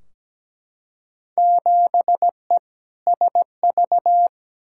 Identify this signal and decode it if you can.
Morse code, heard as a keyed tone, about 17 wpm, 700 Hz. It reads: 7E SV
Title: Field Day Exchanges Encore - 17wpm